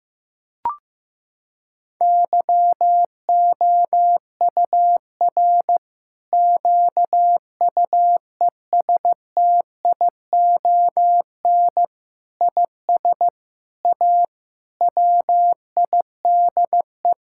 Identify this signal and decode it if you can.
Morse code, heard as a keyed tone, about 15 wpm, 700 Hz. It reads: YOUR QUESTION IS A WIDE
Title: Sentences from Top 1000 Words - Review Part 6 - 15wpm